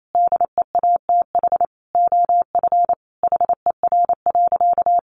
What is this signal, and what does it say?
Morse code, heard as a keyed tone, about 28 wpm, 700 Hz. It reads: DEATH OF HER.
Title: Sentences from Top 1100 Words 28wpm